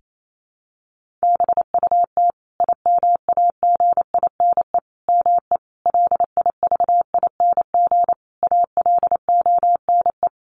Morse code, heard as a keyed tone, about 28 wpm, 700 Hz. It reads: BUT IMAGINE ME LIVING ALONE
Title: Sentences from Top 1400 Words 28wpm